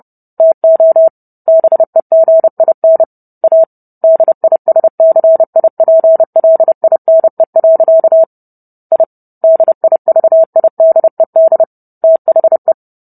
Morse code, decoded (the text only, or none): TO BEGIN A DISCIPLINE. I DIVIDED THE